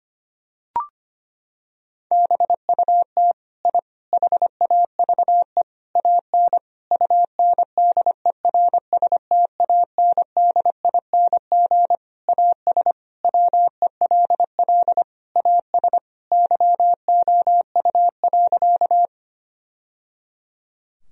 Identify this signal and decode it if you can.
Morse code, heard as a keyed tone, about 25 wpm, 700 Hz. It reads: BUT I HAVE AN UNDERSTANDING AS WELL AS YOU.
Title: Sentences from Top 1000 Words - Review Part 3 - 25wpm